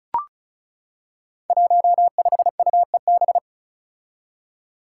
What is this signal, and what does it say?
Morse code, heard as a keyed tone, about 36 wpm, 700 Hz. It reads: 15UEB
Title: Five Challenging Letter-Number Combinations 35wpm